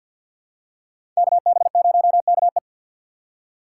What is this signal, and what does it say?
Morse code, heard as a keyed tone, about 49 wpm, 700 Hz. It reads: KB0KE
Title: USA Call Signs - 3x Character Spacing - 50wpm